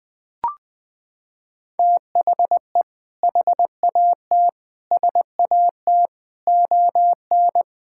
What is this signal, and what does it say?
Morse code, heard as a keyed tone, about 21 wpm, 700 Hz. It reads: THE HAT SAT ON
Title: Sentences with 2-Word Phrases - 20wpm